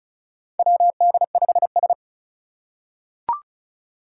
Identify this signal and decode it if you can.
Morse code, heard as a keyed tone, about 36 wpm, 700 Hz. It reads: WD5S
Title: Four Challenging Letter-Number Combinations 35wpm